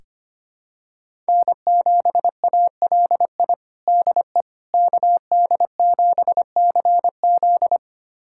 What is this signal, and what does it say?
Morse code, heard as a keyed tone, about 25 wpm, 700 Hz. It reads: N7ALI DE KD7CZ
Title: USA Callsign DE Callsign - 5 Characters - 25wpm